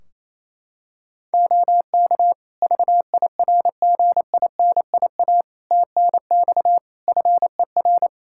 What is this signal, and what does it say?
Morse code, heard as a keyed tone, about 28 wpm, 700 Hz. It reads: OK VIRGINIA TNX FER
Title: Practice QSO - Exchange 4 - Ending - 28wpm